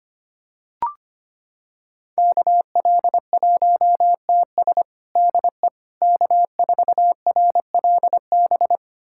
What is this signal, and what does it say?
Morse code, heard as a keyed tone, about 25 wpm, 700 Hz. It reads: KL1TS DE K4RLB
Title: USA Callsign DE Callsign 25wpm